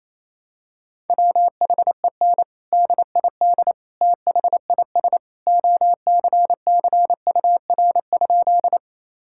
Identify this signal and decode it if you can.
Morse code, heard as a keyed tone, about 28 wpm, 700 Hz. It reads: WHEN DID THIS OCCUR?
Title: Sentences from Top 1000 Words - Review Part 6 - 28wpm